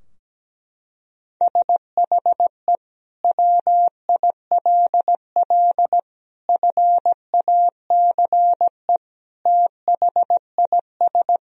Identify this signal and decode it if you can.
Morse code, heard as a keyed tone, about 17 wpm, 700 Hz. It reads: SHE WILL FACE THIS
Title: Sentences from Top 1000 Words - Review Part 2 - 17wpm